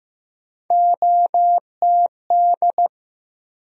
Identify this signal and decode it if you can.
Morse code, heard as a keyed tone, about 15 wpm, 700 Hz. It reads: OTD
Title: USA Call Sign Postfixes - 3 Letters 15wpm